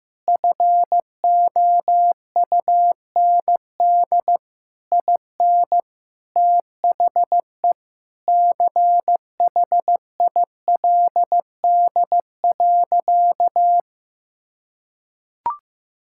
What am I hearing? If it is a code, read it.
Morse code, heard as a keyed tone, about 15 wpm, 700 Hz. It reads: FOUND IN THE CHILD.
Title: Sentences from Top 1000 Words - Review Part 5 - 15wpm